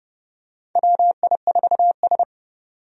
Morse code, heard as a keyed tone, about 32 wpm, 700 Hz. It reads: WI4S
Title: USA Call Signs - 4 Characters - 30wpm